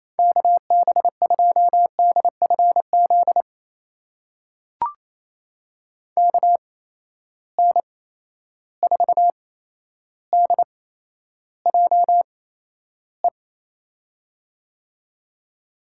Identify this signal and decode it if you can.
Morse code, heard as a keyed tone, about 28 wpm, spaced as for 12 wpm, 700 Hz. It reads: KB2DFZ KN4DJE
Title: USA Call Signs - 6 Characters - 8x Character Spacing - 28wpm